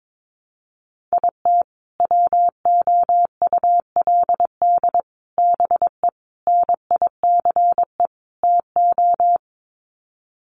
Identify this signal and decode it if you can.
Morse code, heard as a keyed tone, about 22 wpm, 700 Hz. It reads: IT WOULD BE NICE TO
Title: Sets of 5 Words Encore2 22wpm